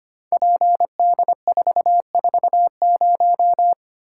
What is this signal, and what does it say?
Morse code, heard as a keyed tone, about 25 wpm, 700 Hz. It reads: PD440
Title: Code Groups - 5 Letter:Numbers - 25wpm